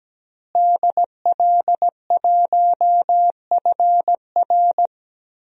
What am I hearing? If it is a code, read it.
Morse code, heard as a keyed tone, about 17 wpm, 700 Hz. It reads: DL1FR
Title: DXCC-100 Call Signs - 5 Characters - Repeated 2x - 17wpm